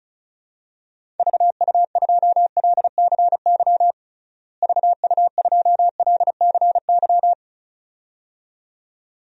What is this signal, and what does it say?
Morse code, heard as a keyed tone, about 34 wpm, 700 Hz. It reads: VU2LCY VU2LCY
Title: DXCC-100 Call Signs - 6 Characters - Repeated 2x - 35wpm